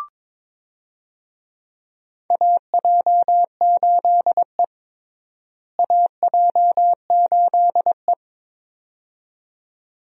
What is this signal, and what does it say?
Morse code, heard as a keyed tone, about 22 wpm, 700 Hz. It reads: AJ8E AJ8E
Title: USA Call Signs - 4 Characters - Repeated 2x - 22wpm